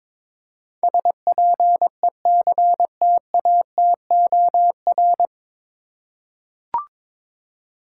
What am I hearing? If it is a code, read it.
Morse code, heard as a keyed tone, about 22 wpm, 700 Hz. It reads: SPECTATOR
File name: 9 Letter Words - First 3 Letters Given - 22wpm